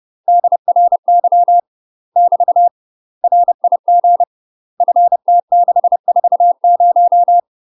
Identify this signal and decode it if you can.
Morse code, heard as a keyed tone, about 30 wpm, 700 Hz. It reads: DRY <BT> RIG FT640
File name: UK MASTER QSOs TRAINING / 30WPM / UK MASTER QSOs 0041